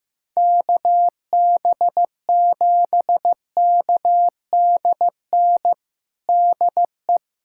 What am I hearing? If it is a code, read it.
Morse code, heard as a keyed tone, about 16 wpm, 700 Hz. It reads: KB7KDN DE
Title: USA Callsign DE Callsign - 6 Characters - 15wpm